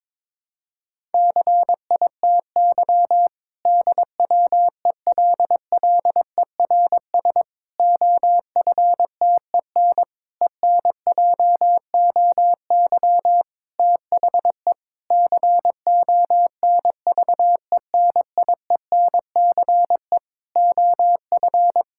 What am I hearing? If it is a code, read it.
Morse code, heard as a keyed tone, about 22 wpm, 700 Hz. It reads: CITY DWELLERS OFTEN ENJOY THE CONVENIENCE OF
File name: Difficult Sentences with Hints - 22wpm